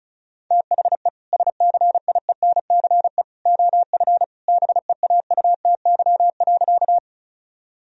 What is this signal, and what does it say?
Morse code, heard as a keyed tone, about 36 wpm, 700 Hz. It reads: THE SCIENCE OF BEAUTY.
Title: Sentences from Top 1000 Words - Review Part 2 - 35wpm